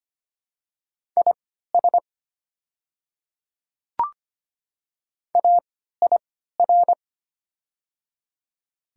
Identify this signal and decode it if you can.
Morse code, heard as a keyed tone, about 25 wpm, spaced as for 14 wpm, 690 Hz. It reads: IS AIR
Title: 3x Character Spacing - Top 200 Words - 25wpm